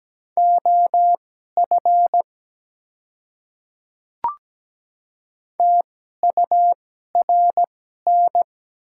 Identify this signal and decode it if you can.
Morse code, heard as a keyed tone, about 17 wpm, spaced as for 12 wpm, 700 Hz. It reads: OF TURN
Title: 2x Character Spacing - Top 300 Words - 17wpm